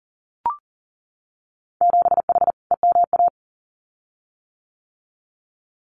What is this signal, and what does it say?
Morse code, heard as a keyed tone, about 42 wpm, 700 Hz. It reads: 7H EMA
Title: Winter Field Day Exchanges Encore 40wpm